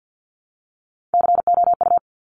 Morse code, heard as a keyed tone, about 50 wpm, 700 Hz. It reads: COU
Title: Top Three Letter Combinations in Words 50wpm